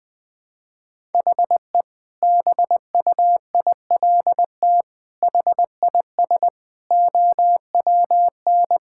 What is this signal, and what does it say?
Morse code, heard as a keyed tone, about 20 wpm, 700 Hz. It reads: HE BUILT HIS OWN
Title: Sentences with 2-Word Phrases - 20wpm